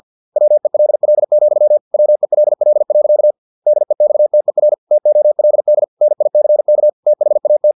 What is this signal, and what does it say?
Morse code, heard as a keyed tone, about 50 wpm, 600 Hz. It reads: WELL, WELL. DEXTER TOLD NICK THAT
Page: Never 49wpm